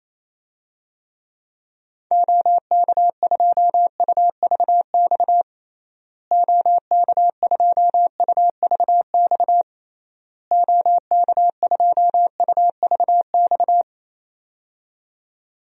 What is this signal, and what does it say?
Morse code, heard as a keyed tone, about 28 wpm, 700 Hz. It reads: OK2UVX OK2UVX OK2UVX
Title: DXCC-100 Call Signs - 6 Characters - Repeated 3x - 28wpm